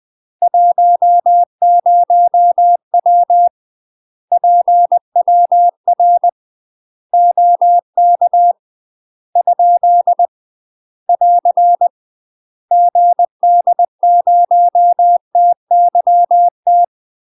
Morse code, heard as a keyed tone, about 20 wpm, 700 Hz. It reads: 10W PWR OK ? <AR> GD0TYT
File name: UK MASTER QSOs TRAINING / 20WPM / UK MASTER QSOs 0071